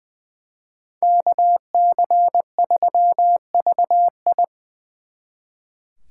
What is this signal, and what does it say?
Morse code, heard as a keyed tone, about 23 wpm, 700 Hz. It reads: KC3VI
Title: Speed Racing - USA Call Signs 20wpm